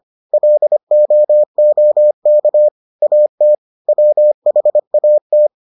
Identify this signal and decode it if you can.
Morse code, heard as a keyed tone, about 25 wpm, 600 Hz. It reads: LOOK AT WHAT